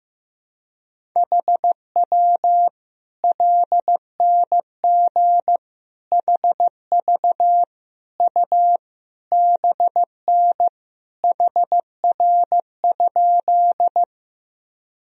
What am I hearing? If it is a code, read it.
Morse code, heard as a keyed tone, about 15 wpm, 700 Hz. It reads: HW LNG HV U BN HR?
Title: Sentences from Top 200 Words and Abbreviations 15wpm